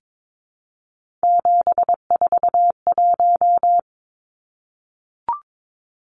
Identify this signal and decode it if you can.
Morse code, heard as a keyed tone, about 22 wpm, 700 Hz. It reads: 741
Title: Numbers - 3-Digit 22wpm